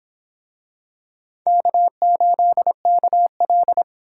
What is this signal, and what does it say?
Morse code, heard as a keyed tone, about 24 wpm, 700 Hz. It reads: K8KL